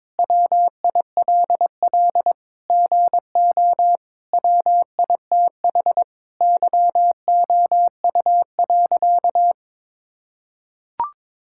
Morse code, heard as a keyed tone, about 22 wpm, 700 Hz. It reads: WILL GO WITH YOU.
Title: Sentences from Top 1600 Words 22wpm